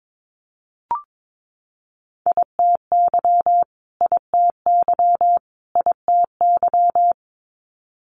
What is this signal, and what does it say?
Morse code, heard as a keyed tone, about 22 wpm, 700 Hz. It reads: ITY ITY ITY
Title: Multi-letter Phonemes Repeated 3x 22wpm